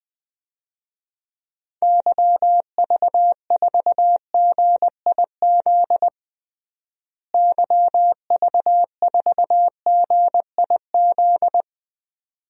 Morse code, heard as a keyed tone, about 20 wpm, 700 Hz. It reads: YV4GIZ YV4GIZ
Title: DXCC-100 Call Signs - 6 Characters - Repeated 2x - 20wpm